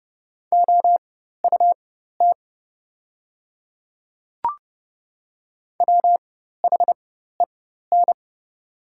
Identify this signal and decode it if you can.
Morse code, heard as a keyed tone, about 30 wpm, spaced as for 14 wpm, 690 Hz. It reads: OUT WHEN
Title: 4x Character Spacing - Top 100 Words - 30wpm